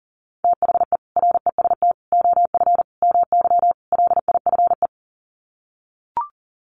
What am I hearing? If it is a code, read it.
Morse code, heard as a keyed tone, about 40 wpm, 700 Hz. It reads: THE REST OF MY LIFE
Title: Sets of 5 Words 40wpm